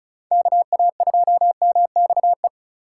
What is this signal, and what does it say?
Morse code, heard as a keyed tone, about 36 wpm, 700 Hz. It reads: KA2MXE
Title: USA Call Signs - 6 Characters - 2x Character Spacing - 35wpm